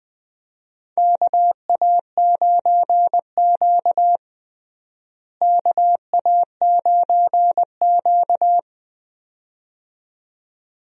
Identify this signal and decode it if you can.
Morse code, heard as a keyed tone, about 20 wpm, 700 Hz. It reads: KA9Q KA9Q
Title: USA Call Signs - 4 Characters - Repeated 2x - 20wpm